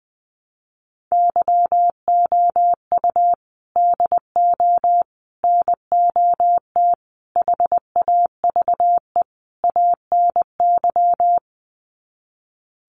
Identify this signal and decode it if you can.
Morse code, heard as a keyed tone, about 20 wpm, 700 Hz. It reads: YOU DO NOT HAVE ANY
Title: Sets of 5 Words Encore 20wpm